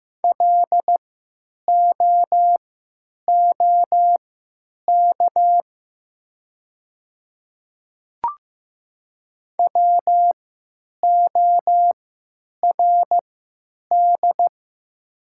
Morse code, heard as a keyed tone, about 15 wpm, spaced as for 8 wpm, 700 Hz. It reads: LOOK WORD
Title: 3x Character Spacing - Top 100 Words - 15wpm